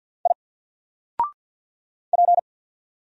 Morse code, heard as a keyed tone, about 48 wpm, 700 Hz. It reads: I P
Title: Single Character (Letter:Number:Punctuation:Prosign) Alt 50wpm